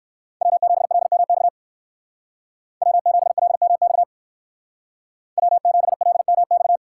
Morse code, heard as a keyed tone, about 58 wpm, 700 Hz. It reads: W7LMX W7LMX W7LMX